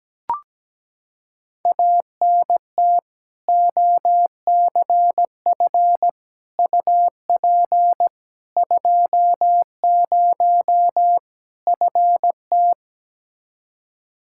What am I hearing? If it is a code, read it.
Morse code, heard as a keyed tone, about 17 wpm, 700 Hz. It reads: ANT OCF UP 20 FT
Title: Practice QSO - Exchange 2 - Info - 17wpm